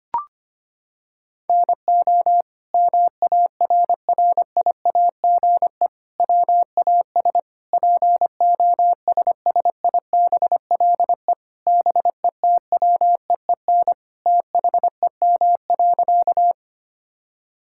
Morse code, heard as a keyed tone, about 25 wpm, 700 Hz. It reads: NO MARRIAGE WAS POSSIBLE BETWEEN THEM.
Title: Sentences from Top 1000 Words - Review Part 5 - 25wpm